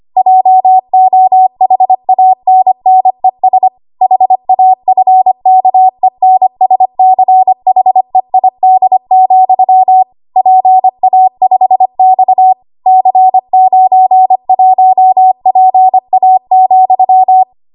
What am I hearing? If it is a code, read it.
Morse code, heard as a keyed tone, about 25 wpm, 750 Hz. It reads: JOHANNES HAFKENSCHEID, PA5X C91PA,
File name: Archive / 25 WPM / 25WPM